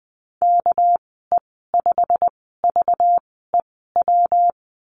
Morse code, heard as a keyed tone, about 20 wpm, spaced as for 15 wpm, 700 Hz. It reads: KE5VEW
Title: 200 Lesson 32g - Letter k - Call Signs 20wpm